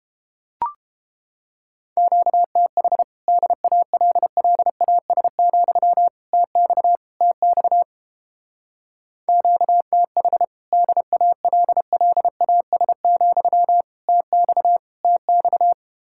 Morse code, heard as a keyed tone, about 32 wpm, 700 Hz. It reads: QTH DALLAS, TX TX QTH DALLAS, TX TX